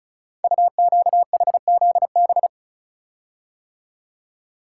Morse code, heard as a keyed tone, about 36 wpm, 690 Hz. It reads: UQHZB
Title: Code Groups - 5 Letters - 35wpm